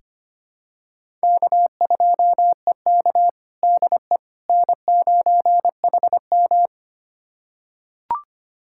Morse code, heard as a keyed tone, about 25 wpm, 700 Hz. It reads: K2EK DE N9HM
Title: USA Callsign DE Callsign - 4 Characters - 25wpm